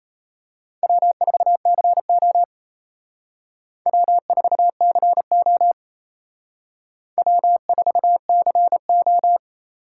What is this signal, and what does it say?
Morse code, heard as a keyed tone, about 32 wpm, 700 Hz. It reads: W4CO W4CO W4CO
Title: Speed Racing - USA 1x2 Call Signs 28wpm